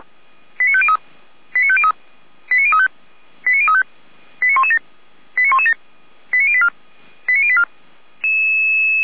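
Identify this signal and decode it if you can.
Five-tone selective call (ZVEI), heard as quick runs of five tones, tones of about 70 ms. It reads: 89503 89503 89035 89035 89117 89117 89084 89084
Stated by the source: ZVEI five-tone selective call received by SDR, code 89080.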